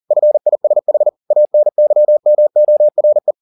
Morse code, heard as a keyed tone, about 40 wpm, 600 Hz. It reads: FISH ANYMORE
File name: the old man and the sea 2 40wpm